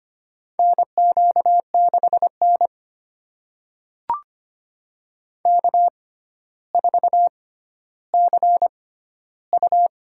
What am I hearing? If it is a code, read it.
Morse code, heard as a keyed tone, about 25 wpm, spaced as for 13 wpm, 700 Hz. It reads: NQ6N K4CU
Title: USA Call Signs - 4 Characters - 6x Character Spacing - 25wpm